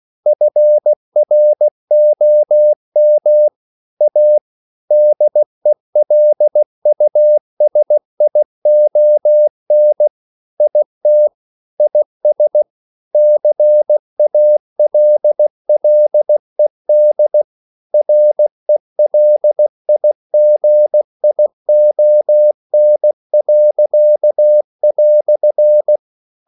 Morse code, heard as a keyed tone, about 16 wpm, 600 Hz. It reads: FROM A DELUSION IT IS CALLED RELIGION."
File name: CW / Practice / EnglishQuotations0116